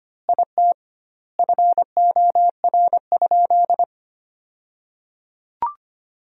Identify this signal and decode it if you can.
Morse code, heard as a keyed tone, about 25 wpm, 700 Hz. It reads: IT FOR?
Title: 2x Word-Spacing - Sentences from Top 100 Words - 25wpm